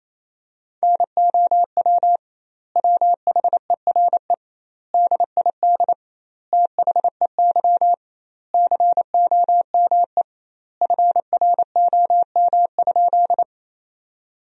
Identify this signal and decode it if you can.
Morse code, heard as a keyed tone, about 28 wpm, 700 Hz. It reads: NOW WHERE DID THEY COME FROM?
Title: 2x Word-Spacing - Sentences from Top 200 Words - 28wpm